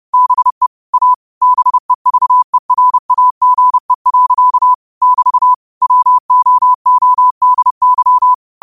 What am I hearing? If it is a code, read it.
Morse code, heard as a keyed tone, about 30 wpm, 1000 Hz. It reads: BE A BEVERAGE. <BT> WOODY
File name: qotdfiles / 30wpm